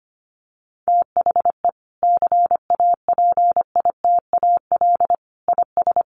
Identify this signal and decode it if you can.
Morse code, heard as a keyed tone, about 25 wpm, 700 Hz. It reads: THE CAPITAL IS
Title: Sentences from Top 1000 Words 25wpm